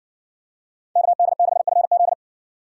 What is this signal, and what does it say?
Morse code, heard as a keyed tone, about 57 wpm, 700 Hz. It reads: KD6FZ